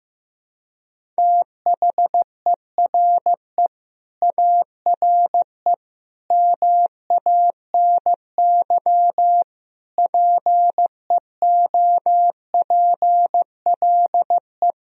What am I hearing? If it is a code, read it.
Morse code, heard as a keyed tone, about 15 wpm, 700 Hz. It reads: THERE ARE MANY PEOPLE
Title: Sentences from Top 1000 Words - Review Part 1 - 15wpm